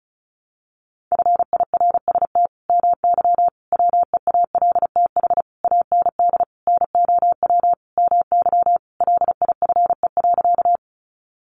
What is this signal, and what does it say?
Morse code, heard as a keyed tone, about 34 wpm, 700 Hz. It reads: FIRST MY WEALTH AND NOW MY LIFE.
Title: Sentences from Top 1200 Words 35wpm